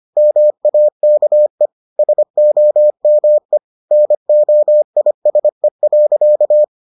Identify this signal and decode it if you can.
Morse code, heard as a keyed tone, about 25 wpm, 600 Hz. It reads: MAKE SOME NOISE.